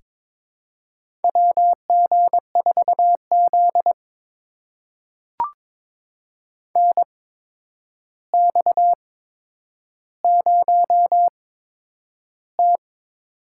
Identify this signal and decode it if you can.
Morse code, heard as a keyed tone, about 22 wpm, spaced as for 9 wpm, 700 Hz. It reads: WG4Z NX0T
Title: USA Call Signs - 4 Characters - 8x Character Spacing - 22wpm